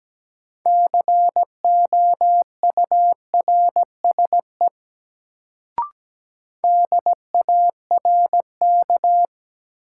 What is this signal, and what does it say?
Morse code, heard as a keyed tone, about 17 wpm, 700 Hz. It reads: COURSE DARK
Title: Sentences from Top 500 Words - New Word Drill - 17wpm